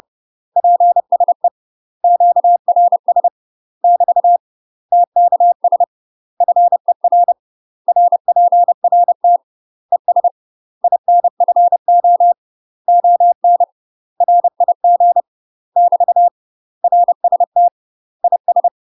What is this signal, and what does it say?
Morse code, heard as a keyed tone, about 30 wpm, 700 Hz. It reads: PSE QRS <BT> TKS FER RPRT ES INFO ON RIG <BT> RST IS
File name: UK MASTER QSOs TRAINING / 30WPM / UK MASTER QSOs 0028